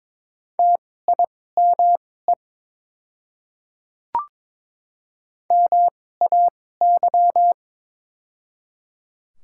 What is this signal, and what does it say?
Morse code, heard as a keyed tone, about 21 wpm, spaced as for 16 wpm, 700 Hz. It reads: TIME MAY
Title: 2x Character Spacing - Top 100 Words - 22wpm